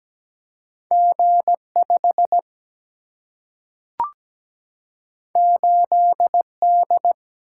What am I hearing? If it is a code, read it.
Morse code, heard as a keyed tone, about 17 wpm, 700 Hz. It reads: G5 8D
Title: Single-Letter Single-Number 17wpm